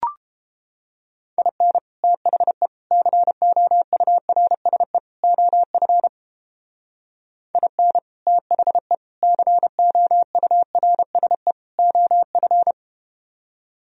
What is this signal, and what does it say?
Morse code, heard as a keyed tone, about 31 wpm, 700 Hz. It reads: IN THE COURSE OF IN THE COURSE OF